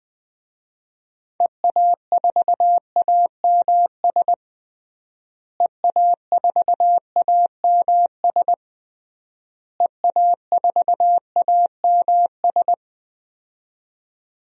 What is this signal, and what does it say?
Morse code, heard as a keyed tone, about 21 wpm, 700 Hz. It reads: EA4AMS EA4AMS EA4AMS
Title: DXCC-100 Call Signs - 6 Characters - Repeated 3x - 20wpm